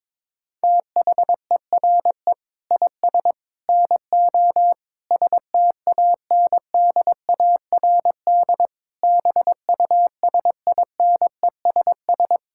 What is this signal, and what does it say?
Morse code, heard as a keyed tone, about 21 wpm, 700 Hz. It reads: THERE IS NO STANDARD BUSINESS
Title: Sentences from Top 1000 Words - Review Part 6 - 22wpm